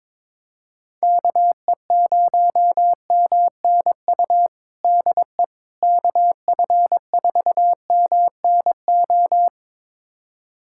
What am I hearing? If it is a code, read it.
Morse code, heard as a keyed tone, about 22 wpm, 700 Hz. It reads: KE0MNU DE KF4MNO
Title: USA Callsign DE Callsign - 6 Characters - 22wpm